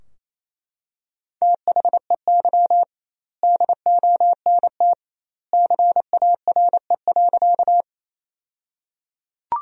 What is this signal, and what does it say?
Morse code, heard as a keyed tone, about 28 wpm, 700 Hz. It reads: THEY DONT CARE.
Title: 2x Word-Spacing - Sentences from Top 300 Words - 28wpm